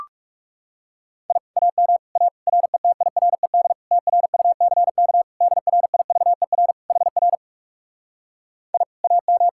Morse code, heard as a keyed tone, about 40 wpm, 700 Hz. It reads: I AM A RETIRED TRUCK DRIVER HR I AM